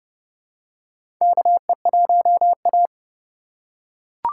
Morse code, heard as a keyed tone, about 30 wpm, 700 Hz. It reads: KE1A
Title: USA Call Signs - 4 Characters - 8x Character Spacing - 30wpm